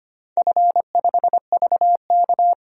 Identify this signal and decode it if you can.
Morse code, heard as a keyed tone, about 25 wpm, 700 Hz. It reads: F5VK
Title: DXCC-100 Call Signs - 4 Characters - 25wpm